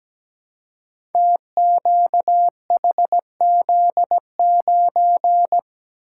Morse code, heard as a keyed tone, about 17 wpm, 700 Hz. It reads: TQHZ9
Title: Code Groups - 5 Letter:Numbers - 17wpm